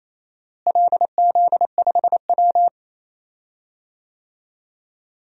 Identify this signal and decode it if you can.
Morse code, heard as a keyed tone, about 28 wpm, 700 Hz. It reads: LZ5W
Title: DXCC-100 Call Signs - 4 Characters - Repeated 2x - 28wpm